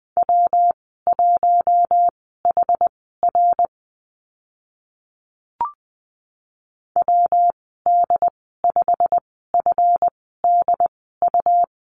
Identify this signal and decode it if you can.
Morse code, heard as a keyed tone, about 20 wpm, spaced as for 15 wpm, 700 Hz. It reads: W1HR WD5FDU
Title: 147 Lesson 24g - Letter y - Call Signs 20wpm